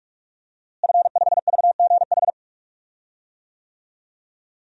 Morse code, heard as a keyed tone, about 46 wpm, 700 Hz. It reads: F5VGH
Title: DXCC-100 Call Signs - 5 Characters - Repeated 3x - 45wpm